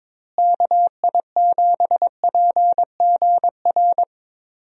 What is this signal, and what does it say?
Morse code, heard as a keyed tone, about 22 wpm, 700 Hz. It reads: KI7PGR
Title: USA Call Signs - 6 Characters - Repeated 3x - 22wpm